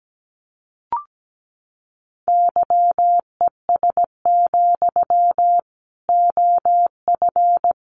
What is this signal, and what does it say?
Morse code, heard as a keyed tone, about 17 wpm, 700 Hz. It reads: YES, OF
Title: Sentences from Top 500 Words 17wpm